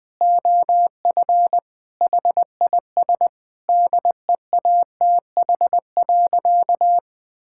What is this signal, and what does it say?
Morse code, heard as a keyed tone, about 20 wpm, 700 Hz. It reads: OF HIS DEATH.
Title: Sentences with 6-Word Phrases - 20wpm